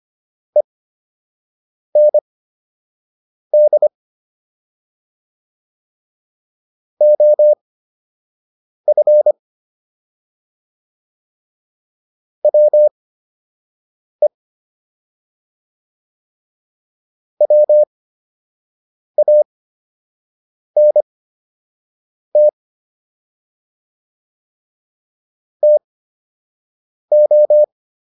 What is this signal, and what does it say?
Morse code, heard as a keyed tone, about 25 wpm, spaced as for 6 wpm, 600 Hz. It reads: END OF WE WANT TO